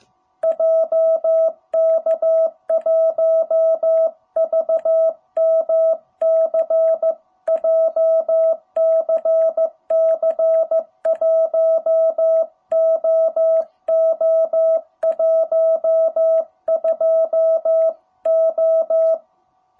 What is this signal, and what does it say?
Morse code, heard as a keyed tone, about 15 wpm, 660 Hz. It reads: JK1VMC JCC 1OO12O
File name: jh8cbh / 2021cw